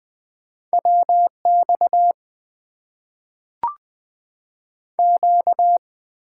Morse code, heard as a keyed tone, about 20 wpm, 700 Hz. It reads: WX Q
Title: Top 90 QSO Elements - 6x Character Spacing - 20wpm